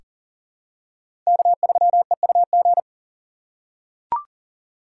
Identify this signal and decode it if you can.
Morse code, heard as a keyed tone, about 40 wpm, 700 Hz. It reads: K3EUG
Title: USA Call Signs - 4x Character Spacing - 40wpm